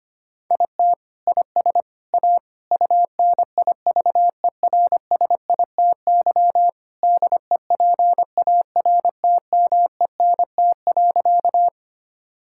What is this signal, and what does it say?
Morse code, heard as a keyed tone, about 25 wpm, 700 Hz. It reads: IT IS A UNIVERSITY DEPARTMENT.
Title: Sentences from Top 1000 Words - Review Part 5 - 25wpm